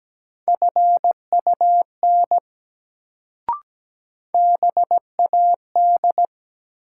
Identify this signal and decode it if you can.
Morse code, heard as a keyed tone, about 17 wpm, 700 Hz. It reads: FUN BAD
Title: Sentences from Top 700 Words - New Word Drill - 17wpm